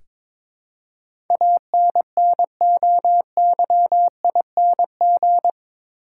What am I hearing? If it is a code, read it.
Morse code, heard as a keyed tone, about 22 wpm, 700 Hz. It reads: ANNOYING
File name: practice / 22wpm / Eight Letter Words - 22wpm